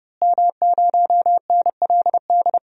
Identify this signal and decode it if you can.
Morse code, heard as a keyed tone, about 30 wpm, 700 Hz. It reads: M0NLD
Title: DXCC-100 Call Signs - 5 Characters - Repeated 2x - 30wpm